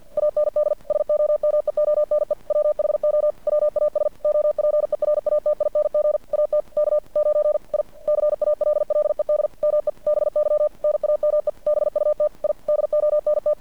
Morse code, heard as a keyed tone, about 50 wpm, 600 Hz. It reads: AND SOMEONE WHO WAS OPERATING AT K9I CALLED ME BY NAME BUT I DONT
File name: archived / CRSnet